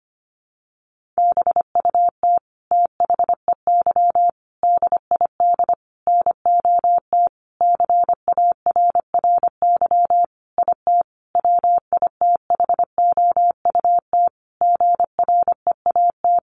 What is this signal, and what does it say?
Morse code, heard as a keyed tone, about 25 wpm, 700 Hz. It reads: BUT THEY DID NOT CARRY IT WITHOUT GREAT
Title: Sentences from Top 1400 Words 25wpm